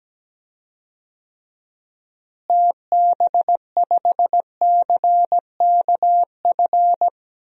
Morse code, heard as a keyed tone, about 17 wpm, 700 Hz. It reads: TB5CKF
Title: DXCC-100 Call Signs - 6 Characters - Repeated 2x - 17wpm